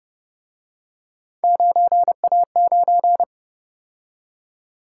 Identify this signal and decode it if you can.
Morse code, heard as a keyed tone, about 30 wpm, 700 Hz. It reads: 9A9
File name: Speed-Racing - Three Letter-Number Combinations - 25wpm